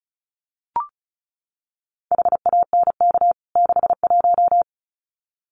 Morse code, heard as a keyed tone, about 34 wpm, 700 Hz. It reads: HANK 61
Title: CWT Exchange 35wpm